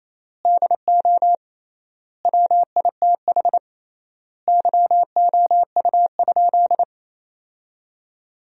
Morse code, heard as a keyed tone, about 28 wpm, 700 Hz. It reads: DO WITH YOU?
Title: 3x Word-Spacing - Sentences from Top 100 Words - 28wpm